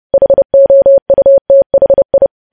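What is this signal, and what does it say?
Morse code, heard as a keyed tone, about 30 wpm, 550 Hz. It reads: HOUTHI
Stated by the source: Morse practice of news headlines at 30wpm.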